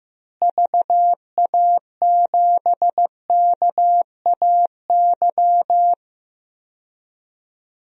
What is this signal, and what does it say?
Morse code, heard as a keyed tone, about 15 wpm, 700 Hz. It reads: VA7KAY
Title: DXCC-100 Call Signs - 6 Characters - Repeated 2x - 15wpm